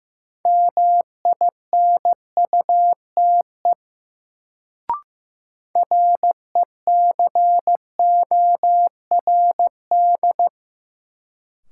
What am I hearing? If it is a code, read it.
Morse code, heard as a keyed tone, about 15 wpm, 700 Hz. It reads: MINUTE RECORD
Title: Sentences from Top 500 Words - New Word Drill - 15wpm